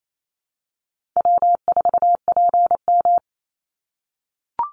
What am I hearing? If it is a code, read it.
Morse code, heard as a keyed tone, about 28 wpm, 700 Hz. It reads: W4PM
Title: Call Signs - 28wpm